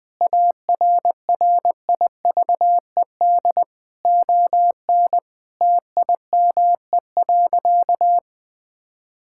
Morse code, heard as a keyed tone, about 21 wpm, 700 Hz. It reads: ARRIVED ON TIME.
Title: Sentences with 4-Word Phrases - 20wpm